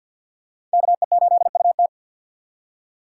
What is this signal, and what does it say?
Morse code, heard as a keyed tone, about 53 wpm, 700 Hz. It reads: KE8UT
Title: USA Call Signs - 2x Character Spacing - 50wpm